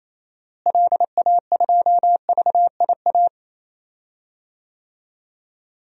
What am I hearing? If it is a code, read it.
Morse code, heard as a keyed tone, about 27 wpm, 700 Hz. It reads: LA2VIA
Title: DXCC-100 Call Signs - 6 Characters - Repeated 3x - 28wpm